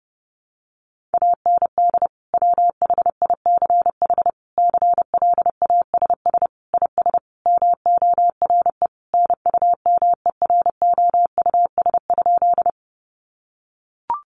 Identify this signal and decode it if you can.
Morse code, heard as a keyed tone, about 30 wpm, 700 Hz. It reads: AND WHICH CLASS IS MORE NUMEROUS?
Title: Sentences from Top 1400 Words 30wpm